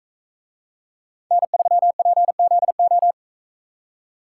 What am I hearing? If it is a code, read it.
Morse code, heard as a keyed tone, about 43 wpm, 700 Hz. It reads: N3PZO